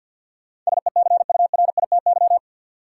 Morse code, heard as a keyed tone, about 53 wpm, 700 Hz. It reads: SECURITY
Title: Sentences from Top 1400 Words - New Word Drill Alt - 50wpm